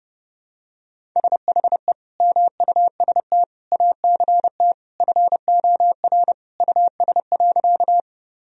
Morse code, heard as a keyed tone, about 30 wpm, 700 Hz. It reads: SHE MUST ACT FOR US.
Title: Sentences from Top 1000 Words - Review Part 1 - 30wpm